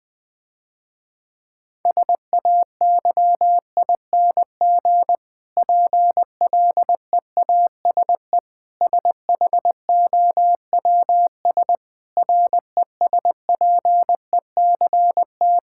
Morse code, heard as a keyed tone, about 20 wpm, 700 Hz. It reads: SAYING PLEASE SHOWS RESPECT
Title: Difficult Sentences with Hints - 20wpm